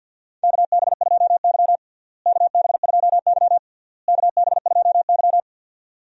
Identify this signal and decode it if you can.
Morse code, heard as a keyed tone, about 52 wpm, 700 Hz. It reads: KB2Y KB2Y KB2Y
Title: USA Call Signs - 4 Characters - Repeated 3x - 50wpm